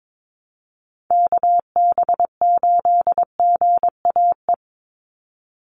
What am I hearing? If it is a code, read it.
Morse code, heard as a keyed tone, about 22 wpm, 700 Hz. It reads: KB8GAE
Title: Call Signs Repeated 3x - 22wpm